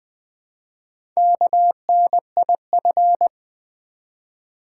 Morse code, heard as a keyed tone, about 20 wpm, 700 Hz. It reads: KNIF
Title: Build A Word - 1 Letter At A Time - 3 to 5-Letter Words - 20wpm